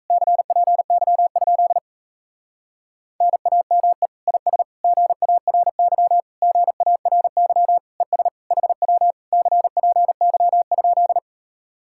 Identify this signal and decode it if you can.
Morse code, heard as a keyed tone, about 38 wpm, 700 Hz. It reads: CPY? NAME IS GARY GARY ES HW CPY?